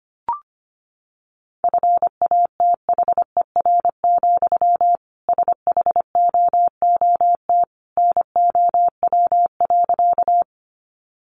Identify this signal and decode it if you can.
Morse code, heard as a keyed tone, about 25 wpm, 700 Hz. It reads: FATHER, SHOOT NOW.
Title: Sentences from Top 1200 Words 25wpm